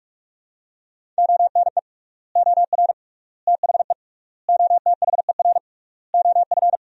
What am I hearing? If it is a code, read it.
Morse code, heard as a keyed tone, about 46 wpm, 700 Hz. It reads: ONE OR THE OTHER OF
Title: 3x Word-Spacing - Sentences from Top 400 Words - 45wpm